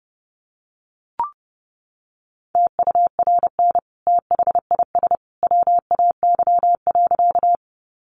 Morse code, heard as a keyed tone, about 30 wpm, 700 Hz. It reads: TURN THIS WAY.
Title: Sentences from Top 200 Words 30wpm